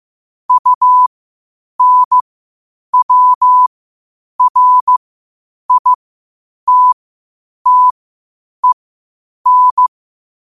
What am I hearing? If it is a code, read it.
Morse code, heard as a keyed tone, about 15 wpm, spaced as for 8 wpm, 1000 Hz. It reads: UNWRITTEN
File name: qotdfiles / 8wpm